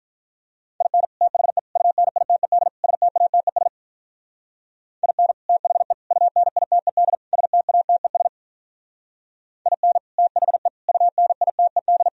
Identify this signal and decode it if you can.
Morse code, heard as a keyed tone, about 46 wpm, 700 Hz. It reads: IN THE UNITED STATES IN THE UNITED STATES IN THE UNITED